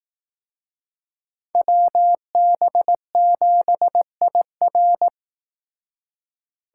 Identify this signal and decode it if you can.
Morse code, heard as a keyed tone, about 19 wpm, 700 Hz. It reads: WB7IR